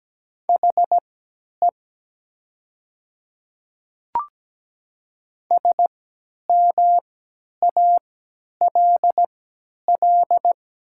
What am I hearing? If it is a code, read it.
Morse code, heard as a keyed tone, about 17 wpm, spaced as for 10 wpm, 700 Hz. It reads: HE SMALL
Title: 3x Character Spacing - Top 200 Words - 17wpm